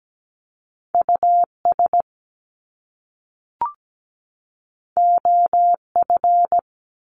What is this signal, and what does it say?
Morse code, heard as a keyed tone, about 17 wpm, 700 Hz. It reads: US OF
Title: Top Two Letter Combinations in Words 17wpm